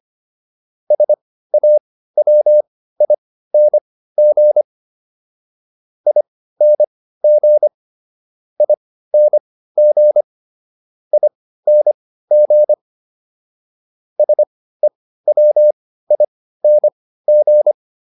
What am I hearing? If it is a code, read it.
Morse code, heard as a keyed tone, about 25 wpm, spaced as for 15 wpm, 600 Hz. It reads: SAWING ING ING ING SEWING